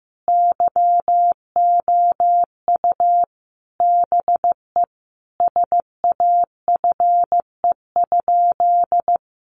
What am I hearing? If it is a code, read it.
Morse code, heard as a keyed tone, about 15 wpm, 700 Hz. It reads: YOU BE SAFE?
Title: Sentences from Top 900 Words 15wpm